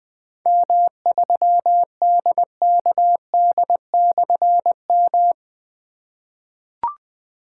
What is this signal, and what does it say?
Morse code, heard as a keyed tone, about 20 wpm, 700 Hz. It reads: M3DKD/M
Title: Difficult Call Signs 20wpm